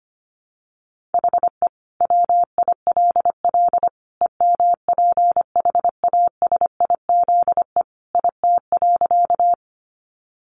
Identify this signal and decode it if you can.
Morse code, heard as a keyed tone, about 25 wpm, 700 Hz. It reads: HE WILL EMPHASIZE IT.
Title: Sentences from Top 1500 Words 25wpm